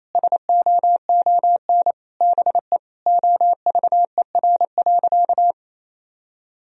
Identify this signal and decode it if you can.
Morse code, heard as a keyed tone, about 28 wpm, 700 Hz. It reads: SOON BE OVER.
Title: Sentences from Top 1000 Words - Review Part 2 - 28wpm